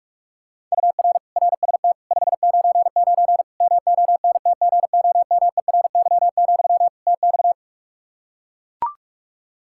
Morse code, heard as a keyed tone, about 42 wpm, 700 Hz. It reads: UR RST 599 MONTGOMERY, TX
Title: K1USN SST QSO Practice 45wpm